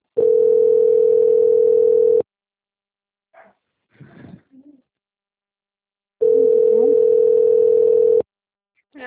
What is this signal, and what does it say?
Ringback tone (North America), 440 + 480 Hz.
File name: out- - - - - .41397